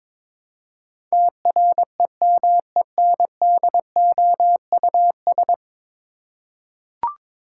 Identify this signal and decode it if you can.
Morse code, heard as a keyed tone, about 22 wpm, 700 Hz. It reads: TREMENDOUS
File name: practice / 22wpm / 10 Letter Words - First 3 Letters Given - 22wpm